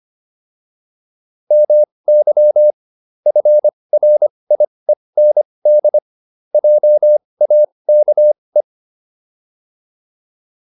Morse code, heard as a keyed tone, about 25 wpm, spaced as for 20 wpm, 600 Hz. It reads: MY FRIEND JAKE